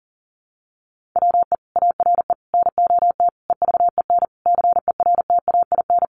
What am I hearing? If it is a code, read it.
Morse code, heard as a keyed tone, about 40 wpm, 700 Hz. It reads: WE ARE NOT EVEN CERTAIN
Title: Sentences from Top 1500 Words 40wpm